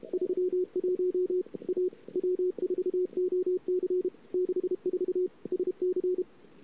Morse code, heard as a keyed tone, about 34 wpm, 410 Hz. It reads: 31 A W4OC 64 SC
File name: CW / 2008-ss-cw-k5tr-op-n5rz -21:00